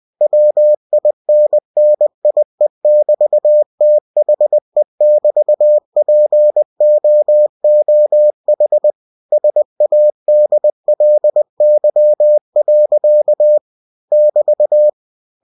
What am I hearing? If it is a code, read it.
Morse code, heard as a keyed tone, about 20 wpm, 600 Hz. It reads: WINNIE<BT>THE<BT>POOH SADLY. <BT>
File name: cw-books / winnie-the-pooh / winnie-20wpm-page-0032